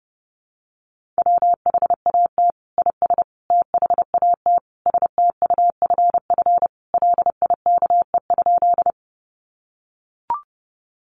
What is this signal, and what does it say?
Morse code, heard as a keyed tone, about 30 wpm, 700 Hz. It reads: WHAT IS THAT STUFF LIKE?
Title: Sentences from Top 1400 Words 30wpm